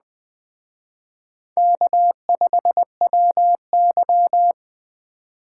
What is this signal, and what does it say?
Morse code, heard as a keyed tone, about 19 wpm, 700 Hz. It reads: K5WY